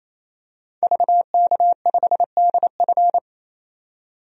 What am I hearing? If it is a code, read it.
Morse code, heard as a keyed tone, about 28 wpm, 700 Hz. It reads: VK5DF
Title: DXCC-100 Call Signs - 5 Characters - 28wpm